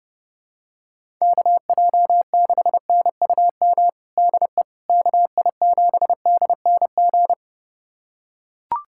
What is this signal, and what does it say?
Morse code, heard as a keyed tone, about 30 wpm, 700 Hz. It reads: KJ6NUM DE KI7DNG
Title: USA Callsign DE Callsign - 6 Characters - 30wpm